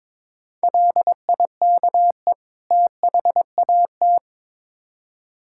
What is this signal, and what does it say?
Morse code, heard as a keyed tone, about 22 wpm, 700 Hz. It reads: LIKE THAT
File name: Build A Phrase - 2-Word Phrases - 22wpm